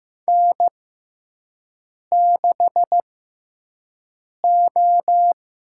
Morse code, heard as a keyed tone, about 15 wpm, spaced as for 5 wpm, 700 Hz. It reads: N6O
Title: USA Call Signs - 5 Characters - 6x Character Spacing - 15wpm